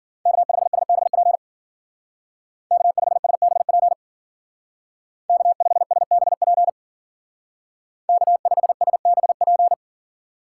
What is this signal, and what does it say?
Morse code, heard as a keyed tone, about 53 wpm, 700 Hz. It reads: K5SBP K5SBP K5SBP K5SBP